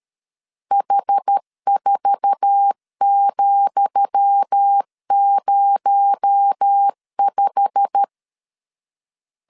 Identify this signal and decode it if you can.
Morse code, heard as a keyed tone, about 13 wpm, 800 Hz. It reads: H4,05
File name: LEZIONI / 10 SIGNS / 10D COMA 12wpm 2544